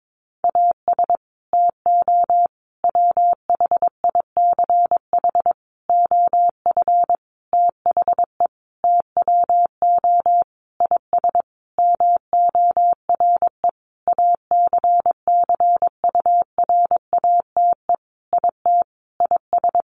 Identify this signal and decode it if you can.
Morse code, heard as a keyed tone, about 22 wpm, 700 Hz. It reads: AS TO WHICH OF THE TWO IS MORE ACCURATE IT IS
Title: Sentences from Top 1200 Words 22wpm